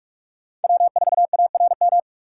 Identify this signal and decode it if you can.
Morse code, heard as a keyed tone, about 44 wpm, 700 Hz. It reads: W4ARM
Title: USA Call Signs - 5 Characters - 3x Character Spacing - 45wpm